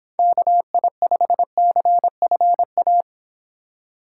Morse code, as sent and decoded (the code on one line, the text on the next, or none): -.- .. ..... -.-. ..-. .-
KI5CFA